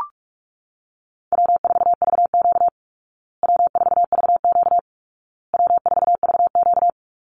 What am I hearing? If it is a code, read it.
Morse code, heard as a keyed tone, about 44 wpm, 700 Hz. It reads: W4VQ W4VQ W4VQ
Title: Easy Call Signs Repeated 3x - 45wpm